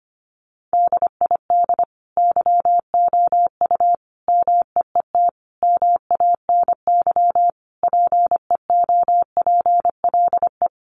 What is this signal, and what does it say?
Morse code, heard as a keyed tone, about 25 wpm, 700 Hz. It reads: DID YOU MEET MANY PEOPLE
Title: Sentences from Top 1100 Words 25wpm